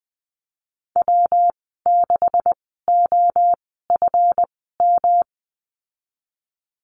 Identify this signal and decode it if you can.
Morse code, heard as a keyed tone, about 20 wpm, spaced as for 15 wpm, 700 Hz. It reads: W6OFM
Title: 147 Lesson 24g - Letter y - Call Signs 20wpm